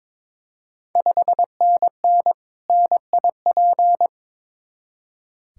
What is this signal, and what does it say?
Morse code, heard as a keyed tone, about 22 wpm, 700 Hz. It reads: 5NN NIP
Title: Ontario QSO Party Exchange 22wpm